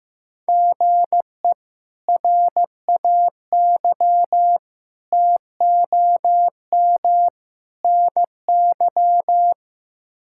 Morse code, heard as a keyed tone, about 16 wpm, 700 Hz. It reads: GE RAY TOM NY
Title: K1USN SST QSO Practice - 15wpm